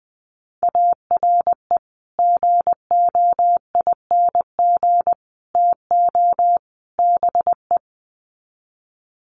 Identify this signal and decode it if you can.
Morse code, heard as a keyed tone, about 20 wpm, 700 Hz. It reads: ARE GOING TO BE
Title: Sets of 4 Words 20wpm